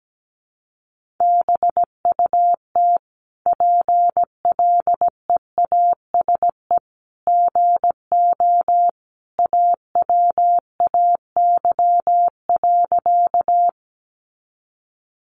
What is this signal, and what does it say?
Morse code, heard as a keyed tone, about 17 wpm, 700 Hz. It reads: BUT PLEASE GO AWAY.
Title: Sentences from Top 1500 Words 17wpm